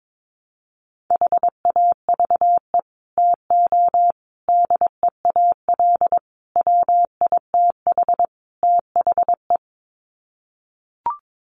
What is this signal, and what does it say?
Morse code, heard as a keyed tone, about 22 wpm, 700 Hz. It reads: HAVE TO DEAL WITH THE
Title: Sets of 5 Words Encore2 22wpm